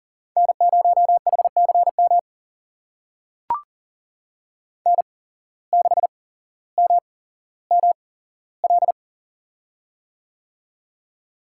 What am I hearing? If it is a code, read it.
Morse code, heard as a keyed tone, about 39 wpm, spaced as for 17 wpm, 700 Hz. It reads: N0HCM N6MML
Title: USA Call Signs - 5 Characters - 8x Character Spacing - 40wpm